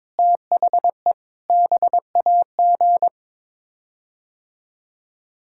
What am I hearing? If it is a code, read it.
Morse code, heard as a keyed tone, about 22 wpm, 700 Hz. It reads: THE BAG
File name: practice / 22wpm / Sets of 7 Words - 22wpm